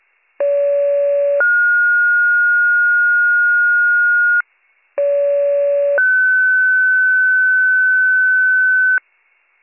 Two-tone page, heard as a one-second tone, then a three-second tone, about 570 then 1430 Hz, about 570 then 1500 Hz. It is heard from 0.4 to 9.0 s.